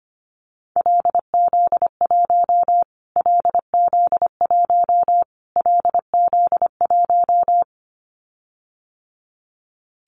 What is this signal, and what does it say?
Morse code, heard as a keyed tone, about 25 wpm, 700 Hz. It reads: LZ1 LZ1 LZ1
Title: Callsign Prefixes for Top 25 DXCC - Repeated 3x - 25wpm